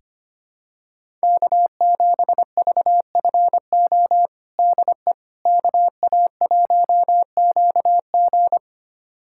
Morse code, heard as a keyed tone, about 25 wpm, 700 Hz. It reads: K7VFO DE KA1QG
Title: USA Callsign DE Callsign - 5 Characters - 25wpm